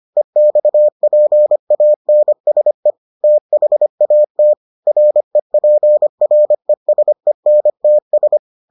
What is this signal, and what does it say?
Morse code, heard as a keyed tone, about 25 wpm, 600 Hz. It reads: EXPANSE THAT REPRESENTS